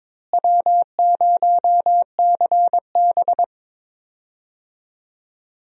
Morse code, heard as a keyed tone, about 22 wpm, 700 Hz. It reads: W0CB
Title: USA Call Signs - 4 Characters - Repeated 3x - 22wpm